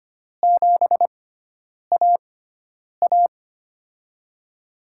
Morse code, heard as a keyed tone, about 25 wpm, spaced as for 9 wpm, 700 Hz. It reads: 7AA
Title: USA Call Signs - 4 Characters - 6x Character Spacing - 25wpm